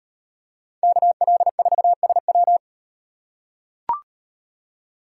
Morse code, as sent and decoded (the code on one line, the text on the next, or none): -.- .-.. ....- ... .--
KL4SW